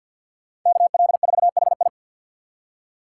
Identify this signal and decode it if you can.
Morse code, heard as a keyed tone, about 53 wpm, 700 Hz. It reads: KL4HI
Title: USA Call Signs - 6x Character Spacing - 50wpm